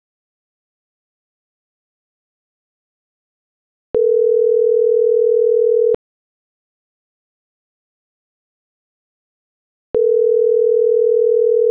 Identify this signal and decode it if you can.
Ringback tone (North America), 440 + 480 Hz.